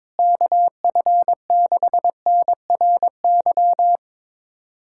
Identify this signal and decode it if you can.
Morse code, heard as a keyed tone, about 22 wpm, 700 Hz. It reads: KF6NRY
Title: USA Callsign DE Callsign 22wpm